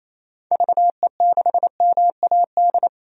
Morse code, heard as a keyed tone, about 28 wpm, 700 Hz. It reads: VE6MAD
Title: DXCC-100 Call Signs - 6 Characters - 28wpm